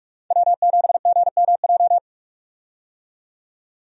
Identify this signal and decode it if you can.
Morse code, heard as a keyed tone, about 46 wpm, 700 Hz. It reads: W7GMJ
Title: USA Call Signs - 5 Characters - 45wpm